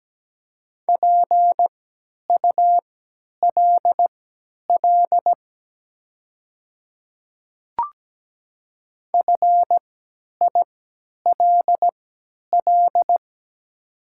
Morse code, heard as a keyed tone, about 17 wpm, spaced as for 10 wpm, 700 Hz. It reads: PULL FILL
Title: 3x Character Spacing - Top 500 Words - 17wpm